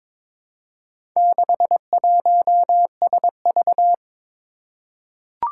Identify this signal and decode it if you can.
Morse code, heard as a keyed tone, about 22 wpm, 700 Hz. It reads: 61SV
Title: Four Challenging Letter-Number Combinations 22wpm